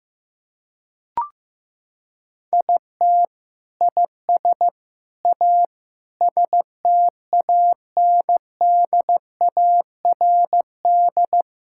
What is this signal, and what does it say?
Morse code, heard as a keyed tone, about 15 wpm, 700 Hz. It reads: IT IS A STANDARD
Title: Sentences from Top 1000 Words - Review Part 6 - 15wpm